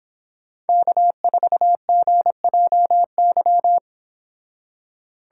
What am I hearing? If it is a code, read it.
Morse code, heard as a keyed tone, about 26 wpm, 700 Hz. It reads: K4GJY